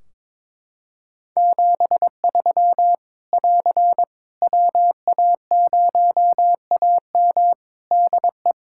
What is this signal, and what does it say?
Morse code, heard as a keyed tone, about 22 wpm, 700 Hz. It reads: 73 <AR> WA0AM DE
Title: K1USN SST QSO Practice 22wpm